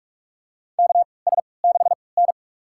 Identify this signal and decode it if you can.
Morse code, heard as a keyed tone, about 46 wpm, spaced as for 25 wpm, 700 Hz. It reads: KS6N
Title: USA Call Signs - 4 Characters - 3x Character Spacing - 45wpm